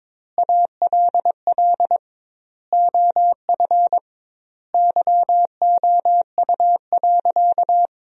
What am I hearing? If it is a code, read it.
Morse code, heard as a keyed tone, about 22 wpm, 700 Hz. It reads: ALL OF YOU.
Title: 2x Word-Spacing - Sentences from Top 500 Words - 22wpm